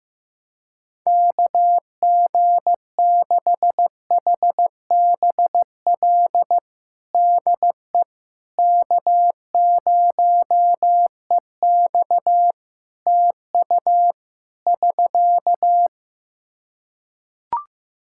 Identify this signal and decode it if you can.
Morse code, heard as a keyed tone, about 15 wpm, 700 Hz. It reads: KG6HBL DE K0EX TU <SK>
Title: K1USN SST QSO Practice 15wpm